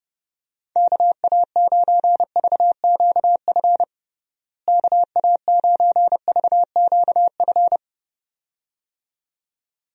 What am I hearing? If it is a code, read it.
Morse code, heard as a keyed tone, about 30 wpm, 700 Hz. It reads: KA9VQF KA9VQF
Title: USA Call Signs - 6 Characters - Repeated 2x - 30wpm